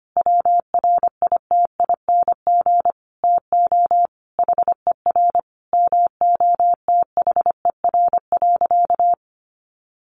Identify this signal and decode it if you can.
Morse code, heard as a keyed tone, about 25 wpm, 700 Hz. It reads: WRITING TO HER MOTHER.
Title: Sentences from Top 800 Words 25wpm